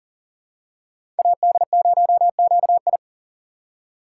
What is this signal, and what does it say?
Morse code, heard as a keyed tone, about 40 wpm, 700 Hz. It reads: AD0QI
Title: USA Call Signs - 5 Characters - 4x Character Spacing - 40wpm